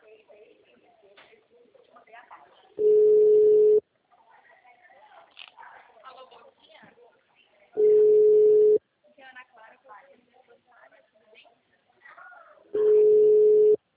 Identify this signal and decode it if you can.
Ringback tone (Europe, 425 Hz), with speech.
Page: Itatiba